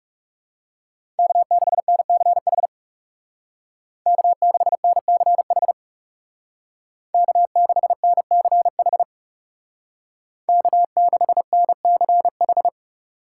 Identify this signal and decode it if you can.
Morse code, heard as a keyed tone, about 35 wpm, 700 Hz. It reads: K6NCH K6NCH K6NCH K6NCH
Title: Speed Racing - USA Call Signs 30wpm